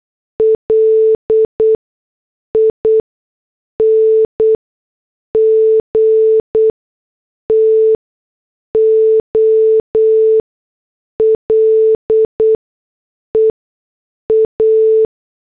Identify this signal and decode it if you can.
Morse code, heard as a keyed tone, about 8 wpm, spaced as for 6 wpm, 440 Hz. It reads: LING TO LEA